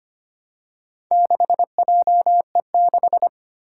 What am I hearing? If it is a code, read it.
Morse code, heard as a keyed tone, about 25 wpm, 700 Hz. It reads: 6JE6
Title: Four Challenging Letter-Number Combinations 25wpm